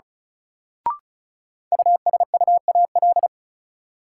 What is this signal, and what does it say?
Morse code, heard as a keyed tone, about 36 wpm, 700 Hz. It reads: USUAL
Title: Sentences from Top 400 Words - New Word Drill - 35wpm